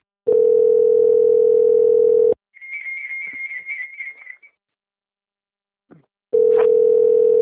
Ringback tone (North America), 440 + 480 Hz.